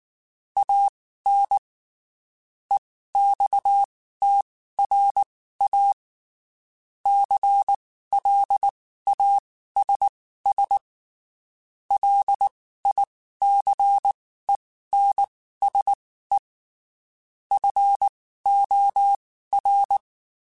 Morse code, heard as a keyed tone, about 19 wpm, spaced as for 14 wpm, 800 Hz. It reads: AN EXTRA CLASS LICENSE FOR
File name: (spaced) / QSOs Spaced-mp3s / kc7haa @18 sp